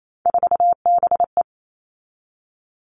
Morse code, heard as a keyed tone, about 28 wpm, 700 Hz. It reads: 4BE
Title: All Cut Numbers - 3-Digit 28wpm